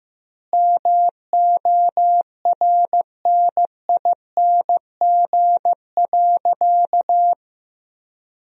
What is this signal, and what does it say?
Morse code, heard as a keyed tone, about 16 wpm, 700 Hz. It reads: MORNING.
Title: 3x Word-Spacing - Sentences from Top 400 Words - 15wpm